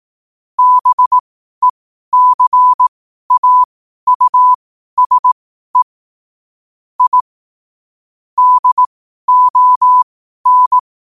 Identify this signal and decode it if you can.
Morse code, heard as a keyed tone, about 18 wpm, spaced as for 12 wpm, 1000 Hz. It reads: BECAUSE I DON
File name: qotdfiles / 12wpm